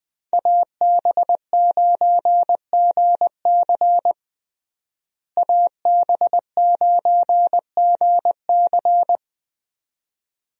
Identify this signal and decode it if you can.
Morse code, heard as a keyed tone, about 20 wpm, 700 Hz. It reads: AB9GC AB9GC
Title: USA Call Signs - 5 Characters - Repeated 2x - 20wpm